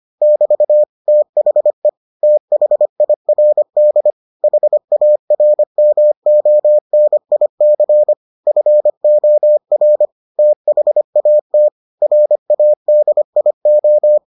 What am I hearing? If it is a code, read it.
Morse code, heard as a keyed tone, about 25 wpm, 600 Hz. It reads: <BT> THE THIRD HARMONIC FOR THAT RADIO